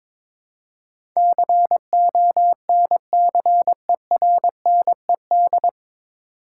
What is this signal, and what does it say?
Morse code, heard as a keyed tone, about 22 wpm, 700 Hz. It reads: CONCERNED
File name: 9 Letter Words - First 3 Letters Given - 22wpm